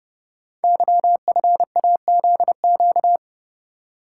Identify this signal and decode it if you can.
Morse code, heard as a keyed tone, about 30 wpm, 700 Hz. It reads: YFAZQ
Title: Code Groups - 5 Letter:Numbers - 30wpm